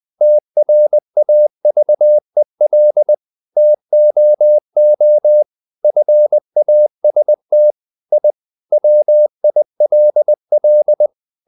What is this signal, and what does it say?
Morse code, heard as a keyed tone, about 20 wpm, 600 Hz. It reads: TRAVEL TOO FAST I WILL
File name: the old man and the sea / the old man and the sea part 1 20wpm